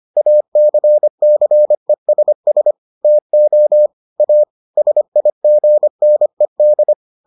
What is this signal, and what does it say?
Morse code, heard as a keyed tone, about 25 wpm, 600 Hz. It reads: ACCESS TO A SIGNED